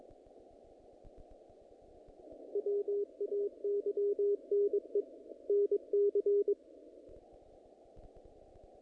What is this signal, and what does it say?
Morse code, heard as a keyed tone, about 22 wpm, 410 Hz. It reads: WAYNE NC
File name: contests / naqp-cw-k5tr-op-k5pi- -2000